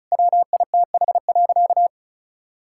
Morse code, heard as a keyed tone, about 36 wpm, 700 Hz. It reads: WITH.
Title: Sentences from Top 1000 Words - Review Part 5 - 35wpm